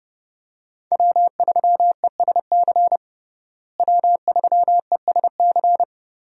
Morse code, heard as a keyed tone, about 30 wpm, 700 Hz. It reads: W3ESC W3ESC
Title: USA Call Signs - 5 Characters - Repeated 2x - 30wpm